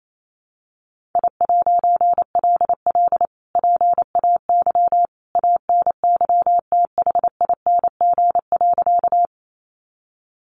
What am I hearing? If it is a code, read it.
Morse code, heard as a keyed tone, about 28 wpm, 700 Hz. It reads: I'LL PAY ANYTHING.
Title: A Sound Of Thunder 28wpm